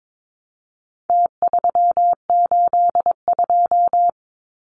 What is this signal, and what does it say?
Morse code, heard as a keyed tone, about 22 wpm, 700 Hz. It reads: T382
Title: Numbers - Contest Style - 4-Digit 22wpm